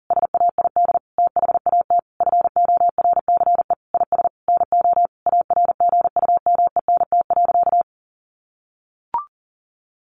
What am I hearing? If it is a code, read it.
Morse code, heard as a keyed tone, about 40 wpm, 700 Hz. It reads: SAID THAT FORCE IS NO ARGUMENT.
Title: Sentences from Top 1400 Words 40wpm